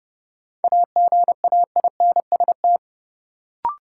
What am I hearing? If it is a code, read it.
Morse code, heard as a keyed tone, about 30 wpm, 700 Hz. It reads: AGAINST
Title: Sets of 2 Words - New Word Drill - 30wpm